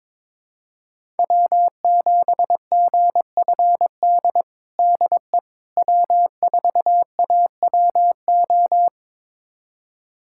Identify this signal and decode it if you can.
Morse code, heard as a keyed tone, about 21 wpm, 700 Hz. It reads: W7GFD DE W4AWO
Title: USA Callsign DE Callsign - 5 Characters - 22wpm